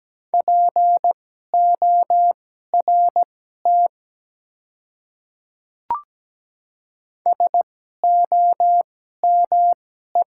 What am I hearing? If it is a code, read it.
Morse code, heard as a keyed tone, about 17 wpm, spaced as for 12 wpm, 700 Hz. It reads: PORT SOME
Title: 2x Character Spacing - Top 200 Words - 17wpm